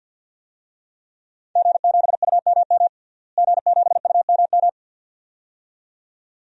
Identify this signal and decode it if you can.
Morse code, heard as a keyed tone, about 52 wpm, 700 Hz. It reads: G7UMM G7UMM
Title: DXCC-100 Call Signs - 5 Characters - Repeated 2x - 50wpm